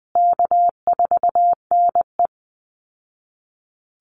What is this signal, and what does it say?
Morse code, heard as a keyed tone, about 20 wpm, 700 Hz. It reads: K4NE
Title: Easy Call Signs Repeated 2x - 20wpm